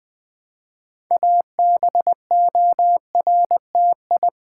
Morse code, heard as a keyed tone, about 20 wpm, 700 Hz. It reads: ABORTI
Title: Build A Word - 2 Letters At A Time - 8-Letter Words - 20wpm